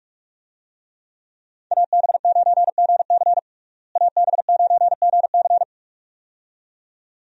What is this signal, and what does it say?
Morse code, heard as a keyed tone, about 46 wpm, 700 Hz. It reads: AB9GC AB9GC
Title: USA Call Signs - 5 Characters - Repeated 2x - 45wpm